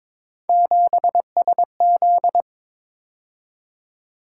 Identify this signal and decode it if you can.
Morse code, heard as a keyed tone, about 26 wpm, 700 Hz. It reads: 7SZ
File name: Speed-Racing - Three Letter-Number Combinations - 22wpm